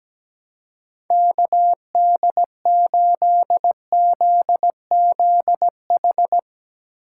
Morse code, heard as a keyed tone, about 17 wpm, 700 Hz. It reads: KD8ZZH
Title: USA Call Signs - 6 Characters - 2x Character Spacing - 17wpm